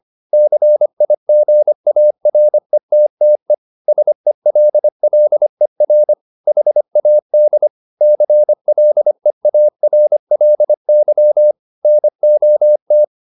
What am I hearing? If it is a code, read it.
Morse code, heard as a keyed tone, about 25 wpm, 600 Hz. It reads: CIGARETTE SELLER HAD CLEARLY NOT